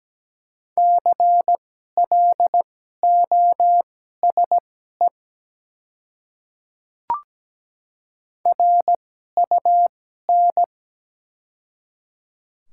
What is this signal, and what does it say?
Morse code, heard as a keyed tone, about 17 wpm, spaced as for 12 wpm, 700 Hz. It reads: CLOSE RUN
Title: 2x Character Spacing - Top 300 Words - 17wpm